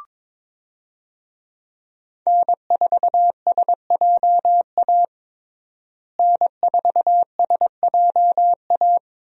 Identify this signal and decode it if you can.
Morse code, heard as a keyed tone, about 21 wpm, 700 Hz. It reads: N4SJA N4SJA
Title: USA Call Signs - 5 Characters - Repeated 2x - 22wpm